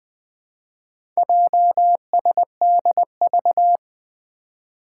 Morse code, heard as a keyed tone, about 20 wpm, 700 Hz. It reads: JSDV
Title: Four Challenging Letter-Number Combinations 20wpm